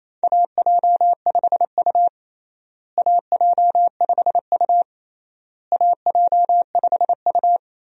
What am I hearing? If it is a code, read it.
Morse code, heard as a keyed tone, about 27 wpm, 700 Hz. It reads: AJ5U AJ5U AJ5U
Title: USA Call Signs - 4 Characters - Repeated 3x - 28wpm